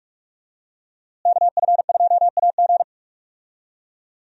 Morse code, heard as a keyed tone, about 44 wpm, 700 Hz. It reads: KF2AG
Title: USA Call Signs - 5 Characters - Repeated 3x - 45wpm